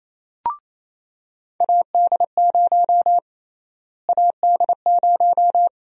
Morse code, heard as a keyed tone, about 28 wpm, 700 Hz. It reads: AD0 AD0
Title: USA Call Sign Prefixes - Repeated 2x - 28wpm